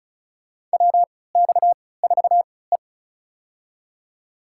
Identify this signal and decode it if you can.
Morse code, heard as a keyed tone, about 36 wpm, spaced as for 20 wpm, 700 Hz. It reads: WX4E
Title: USA Call Signs - 4 Characters - 3x Character Spacing - 35wpm